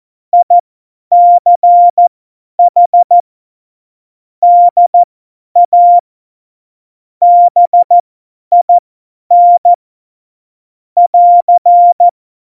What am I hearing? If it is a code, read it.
Morse code, heard as a keyed tone, about 14 wpm, spaced as for 10 wpm, 700 Hz. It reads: ICH DA BIN <AR>
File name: Morsekurs / Witze / Witz 15. Dezember 2020, 50 F70